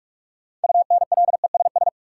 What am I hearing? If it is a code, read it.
Morse code, heard as a keyed tone, about 46 wpm, 700 Hz. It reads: UNLESS
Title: Sentences from Top 1000 Words - Review Part 5 - New Word Drill - 45wpm